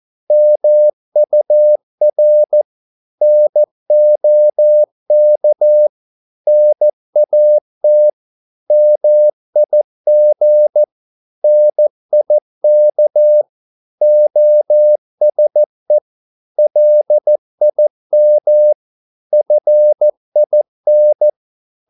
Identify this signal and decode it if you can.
Morse code, heard as a keyed tone, about 14 wpm, 600 Hz. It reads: MUR NOK NAT MIG NIK OSE LIM FIN
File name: Korte ord DK 14wpm